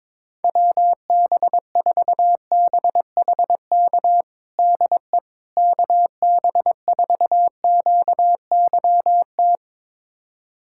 Morse code, heard as a keyed tone, about 22 wpm, 700 Hz. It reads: WB4BHK DE KB4QYT
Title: USA Callsign DE Callsign - 6 Characters - 22wpm